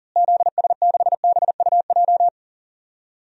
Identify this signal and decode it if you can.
Morse code, heard as a keyed tone, about 40 wpm, 700 Hz. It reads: ZS6BUJ
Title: DXCC-100 Call Signs - 6 Characters - Repeated 3x - 40wpm